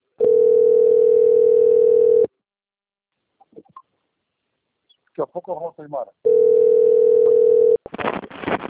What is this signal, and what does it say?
Ringback tone (North America), 440 + 480 Hz, with speech.
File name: rg-1600-unknown- - - .23663